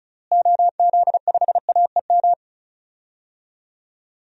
Morse code, heard as a keyed tone, about 34 wpm, 700 Hz. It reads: OZ5AEM
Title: DXCC-100 Call Signs - 6 Characters - Repeated 3x - 35wpm